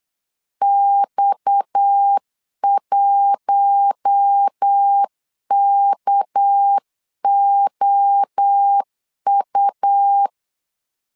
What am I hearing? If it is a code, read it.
Morse code, heard as a keyed tone, about 8 wpm, 800 Hz. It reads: X1KOU
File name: CW / DET / 1 6 1500C S C 8wpm 5219